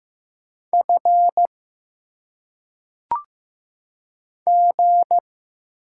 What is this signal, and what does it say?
Morse code, heard as a keyed tone, about 15 wpm, 700 Hz. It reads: F G
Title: Single Letters - Phonetic Pronunciation 15wpm